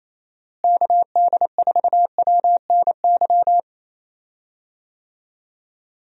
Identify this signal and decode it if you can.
Morse code, heard as a keyed tone, about 28 wpm, 700 Hz. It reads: KD4WNY
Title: USA Call Signs - 6 Characters - Repeated 2x - 28wpm